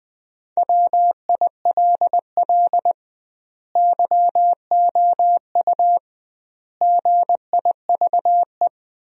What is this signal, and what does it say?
Morse code, heard as a keyed tone, about 21 wpm, 700 Hz. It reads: WILL YOU GIVE
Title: 2x Word-Spacing - Sentences from Top 200 Words - 20wpm